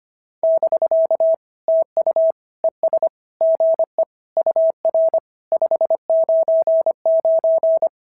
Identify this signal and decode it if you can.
Morse code, heard as a keyed tone, about 25 wpm, 690 Hz. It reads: <BK> TU ES GE UR 599
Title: SOTA QSO Practice 25wpm